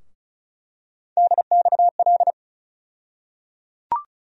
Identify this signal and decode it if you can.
Morse code, heard as a keyed tone, about 36 wpm, 700 Hz. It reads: DXL
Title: USA Call Sign Postfixes - 3 Letters 35wpm